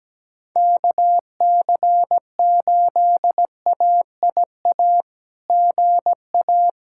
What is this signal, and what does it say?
Morse code, heard as a keyed tone, about 17 wpm, 700 Hz. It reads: KC8AIA GA
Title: SOTA QSO Practice 17wpm